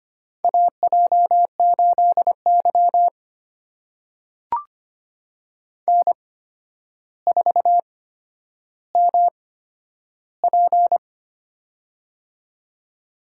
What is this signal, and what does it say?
Morse code, heard as a keyed tone, about 25 wpm, spaced as for 11 wpm, 700 Hz. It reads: AJ8Y N4MP
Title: USA Call Signs - 4 Characters - 8x Character Spacing - 25wpm